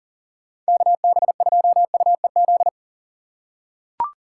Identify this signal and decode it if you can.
Morse code, heard as a keyed tone, about 40 wpm, 700 Hz. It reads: KB2UEZ
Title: USA Call Signs - 6 Characters - 3x Character Spacing - 40wpm